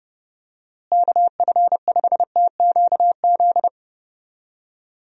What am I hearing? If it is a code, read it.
Morse code, heard as a keyed tone, about 30 wpm, 700 Hz. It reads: KF5TQZ
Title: USA Call Signs - 6 Characters - Repeated 2x - 30wpm